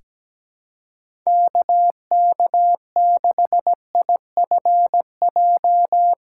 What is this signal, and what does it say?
Morse code, heard as a keyed tone, about 17 wpm, 700 Hz. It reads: KK6IFJ
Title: USA Call Signs - 6 Characters - 6x Character Spacing - 17wpm